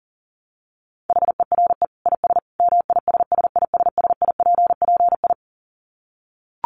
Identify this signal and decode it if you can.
Morse code, heard as a keyed tone, about 40 wpm, 700 Hz. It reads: HERE IS MISSISSIPPI
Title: Dit Challenge 40wpm